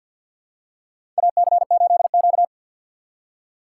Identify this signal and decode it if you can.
Morse code, heard as a keyed tone, about 52 wpm, 700 Hz. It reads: AC8Q
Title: USA Call Signs - 6x Character Spacing - 50wpm